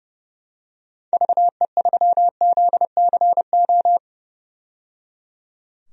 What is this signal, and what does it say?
Morse code, heard as a keyed tone, about 30 wpm, 700 Hz. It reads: VE3ZCO
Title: DXCC-100 Call Signs - 6 Characters - 30wpm